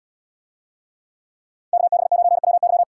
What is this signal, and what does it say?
Morse code, heard as a keyed tone, about 73 wpm, 700 Hz. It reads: KB9WY